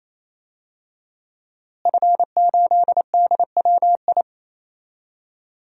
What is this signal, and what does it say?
Morse code, heard as a keyed tone, about 28 wpm, 700 Hz. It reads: F8DWI
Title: DXCC-100 Call Signs - 5 Characters - Repeated 3x - 28wpm